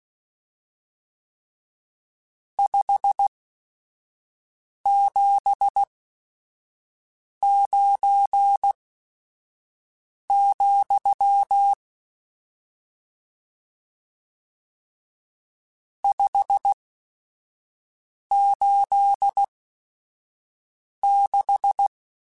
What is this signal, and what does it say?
Morse code, heard as a keyed tone, about 16 wpm, spaced as for 5 wpm, 800 Hz. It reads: 579, 586
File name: RSTs / RST-mp3s / Various RST @5